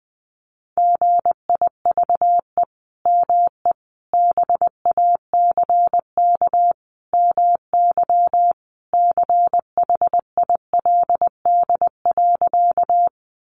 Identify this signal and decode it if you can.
Morse code, heard as a keyed tone, about 20 wpm, 700 Hz. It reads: GIVE ME BACK MY CHILD.
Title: Sentences from Top 800 Words 20wpm